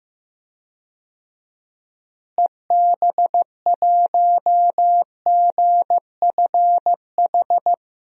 Morse code, heard as a keyed tone, about 15 wpm, 700 Hz. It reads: EB1GFH
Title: DXCC-100 Call Signs - 6 Characters - Repeated 3x - 15wpm